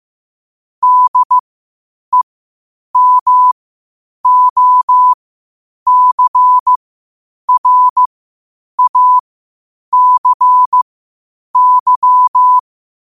Morse code, heard as a keyed tone, about 15 wpm, spaced as for 8 wpm, 1000 Hz. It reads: DEMOCRACY
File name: qotdfiles / 8wpm